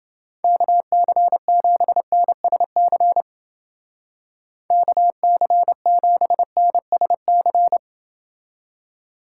Morse code, heard as a keyed tone, about 30 wpm, 700 Hz. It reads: KC7NSC KC7NSC